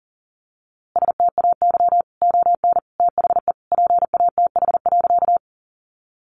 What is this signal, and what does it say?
Morse code, heard as a keyed tone, about 40 wpm, 700 Hz. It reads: STAY ON THE PATH.
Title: A Sound Of Thunder 40wpm